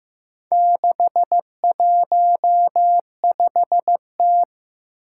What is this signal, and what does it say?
Morse code, heard as a keyed tone, about 15 wpm, 700 Hz. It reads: 615T
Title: Four Challenging Letter-Number Combinations 15wpm